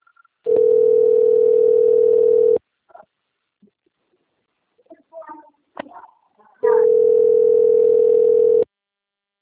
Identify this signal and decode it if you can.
Ringback tone (North America), 440 + 480 Hz.
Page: Atibaia - SP 1 (CP Cartao)